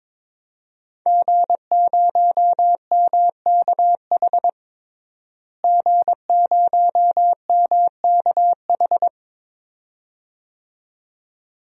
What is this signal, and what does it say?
Morse code, heard as a keyed tone, about 22 wpm, 700 Hz. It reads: G0MKH G0MKH
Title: DXCC-100 Call Signs - 5 Characters - Repeated 2x - 22wpm